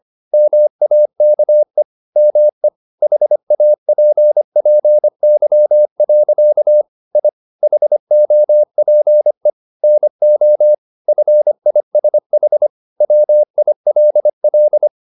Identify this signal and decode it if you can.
Morse code, heard as a keyed tone, about 25 wpm, 600 Hz. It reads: MAKE ME HAPPY. I HOPE NO FISH WILL
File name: the old man and the sea part 1 25wpm